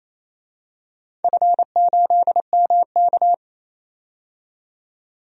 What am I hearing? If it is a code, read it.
Morse code, heard as a keyed tone, about 29 wpm, 700 Hz. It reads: F8MK
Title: DXCC-100 Call Signs - 4 Characters - 28wpm